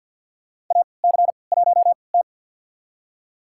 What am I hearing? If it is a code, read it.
Morse code, heard as a keyed tone, about 50 wpm, spaced as for 28 wpm, 700 Hz. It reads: AC1T
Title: USA Call Signs - 4 Characters - 3x Character Spacing - 50wpm